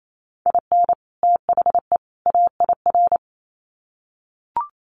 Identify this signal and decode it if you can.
Morse code, heard as a keyed tone, about 28 wpm, 700 Hz. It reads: IN THE AIR
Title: Sets of 3 Words 28wpm